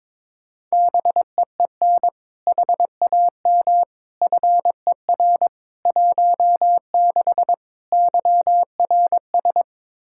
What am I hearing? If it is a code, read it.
Morse code, heard as a keyed tone, about 22 wpm, 700 Hz. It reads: BEEN HAM FER 16 YRS
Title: K1USN SST QSO Practice 22wpm